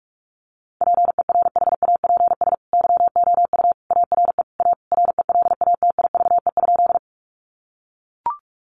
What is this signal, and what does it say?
Morse code, heard as a keyed tone, about 44 wpm, 700 Hz. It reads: PERHAPS YOU ARE A RELATIVE?
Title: Sentences from Top 1400 Words 45wpm